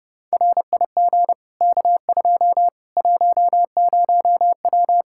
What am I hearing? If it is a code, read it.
Morse code, heard as a keyed tone, about 32 wpm, 700 Hz. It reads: RIG K2 10W
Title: Practice QSO - Exchange 2 - Info - 30wpm